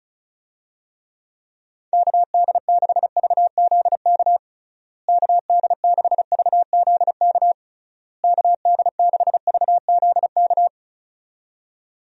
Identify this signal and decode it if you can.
Morse code, heard as a keyed tone, about 36 wpm, 700 Hz. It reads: KD6VZK KD6VZK KD6VZK
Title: USA Call Signs - 6 Characters - Repeated 3x - 35wpm